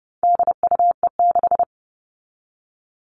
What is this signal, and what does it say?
Morse code, heard as a keyed tone, about 30 wpm, 700 Hz. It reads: DUE6
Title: All Cut Numbers - 4-Digit 30wpm